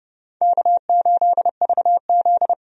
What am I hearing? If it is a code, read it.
Morse code, heard as a keyed tone, about 31 wpm, 700 Hz. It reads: K8VZ
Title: USA Call Signs - 4 Characters - 30wpm